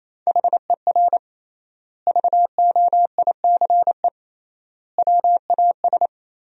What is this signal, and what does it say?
Morse code, heard as a keyed tone, about 28 wpm, 700 Hz. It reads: HER VOICE WAS
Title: 3x Word-Spacing - Sentences from Top 500 Words - 28wpm